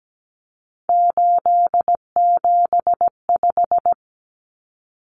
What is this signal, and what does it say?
Morse code, heard as a keyed tone, about 17 wpm, 700 Hz. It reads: 875
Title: Numbers - 3-Digit 17wpm